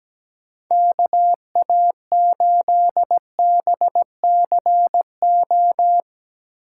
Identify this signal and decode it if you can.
Morse code, heard as a keyed tone, about 17 wpm, 700 Hz. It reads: KA8BCO
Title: USA Call Signs - 2x Character Spacing - 17wpm